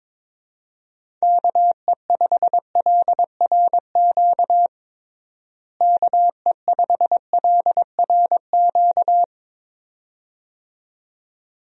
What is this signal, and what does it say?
Morse code, heard as a keyed tone, about 22 wpm, 700 Hz. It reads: KE5LRQ KE5LRQ
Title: USA Call Signs - 6 Characters - Repeated 2x - 22wpm